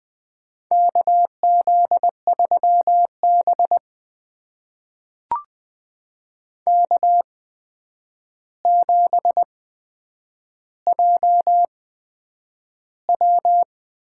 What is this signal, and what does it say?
Morse code, heard as a keyed tone, about 20 wpm, spaced as for 9 wpm, 700 Hz. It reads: KZ3B K7JW
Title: USA Call Signs - 4 Characters - 8x Character Spacing - 20wpm